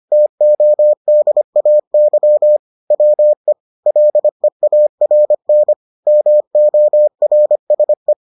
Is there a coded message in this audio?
Yes: Morse code from 0.1 to 8.1 s, heard as a keyed tone, about 25 wpm, 600 Hz. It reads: TODAY WE LEARN MORSE